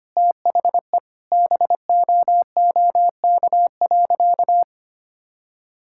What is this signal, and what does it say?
Morse code, heard as a keyed tone, about 25 wpm, 700 Hz. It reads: THE BOOK.
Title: Sentences from Top 1000 Words - Review Part 5 - 25wpm